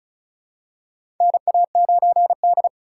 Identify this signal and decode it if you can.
Morse code, heard as a keyed tone, about 36 wpm, 700 Hz. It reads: NA9D
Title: USA Call Signs - 4 Characters - 2x Character Spacing - 35wpm